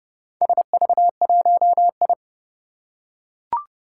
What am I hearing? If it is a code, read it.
Morse code, heard as a keyed tone, about 29 wpm, 700 Hz. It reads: SV1I
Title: Four Challenging Letter-Number Combinations 30wpm